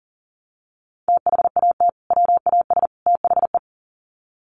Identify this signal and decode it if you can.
Morse code, heard as a keyed tone, about 40 wpm, 700 Hz. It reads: THAT WAS THE
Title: Sets of 3 Words 40wpm